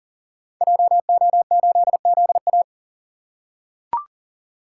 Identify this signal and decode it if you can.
Morse code, heard as a keyed tone, about 40 wpm, 700 Hz. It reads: JO8ZA
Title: Code Groups - 5 Letter:Numbers - 40wpm